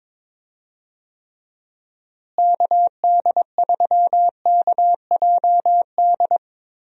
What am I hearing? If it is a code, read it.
Morse code, heard as a keyed tone, about 22 wpm, 700 Hz. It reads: KD3KJD
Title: USA Call Signs - 6 Characters - Repeated 2x - 22wpm